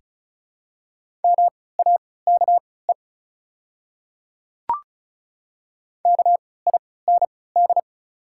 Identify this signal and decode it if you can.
Morse code, heard as a keyed tone, about 36 wpm, spaced as for 20 wpm, 700 Hz. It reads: MAKE KIND
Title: 3x Character Spacing - Top 200 Words - 35wpm